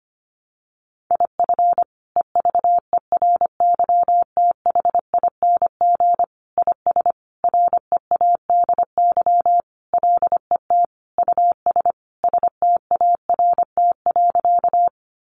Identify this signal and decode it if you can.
Morse code, heard as a keyed tone, about 25 wpm, 700 Hz. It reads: IF EVERYTHING IS READY LET US START.
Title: Sentences from Top 700 Words 25wpm